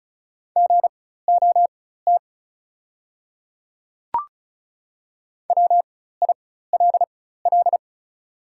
Morse code, heard as a keyed tone, about 36 wpm, spaced as for 16 wpm, 690 Hz. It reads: GOT WILL
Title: 4x Character Spacing - Top 300 Words - 35wpm